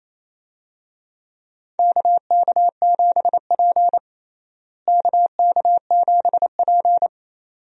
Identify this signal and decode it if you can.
Morse code, heard as a keyed tone, about 28 wpm, 700 Hz. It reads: KK7P KK7P
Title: USA Call Signs - 4 Characters - Repeated 2x - 28wpm